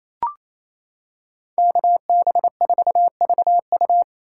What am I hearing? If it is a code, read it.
Morse code, heard as a keyed tone, about 28 wpm, 700 Hz. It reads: KB4VU
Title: K1USN SST QSO Practice - 28wpm